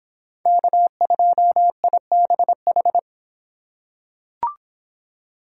Morse code, heard as a keyed tone, about 24 wpm, 700 Hz. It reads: K2IBH